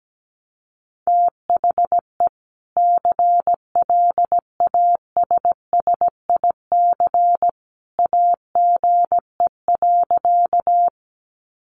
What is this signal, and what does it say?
Morse code, heard as a keyed tone, about 17 wpm, 700 Hz. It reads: THE CLASSIC AGE.
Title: Sentences from Top 1000 Words 17wpm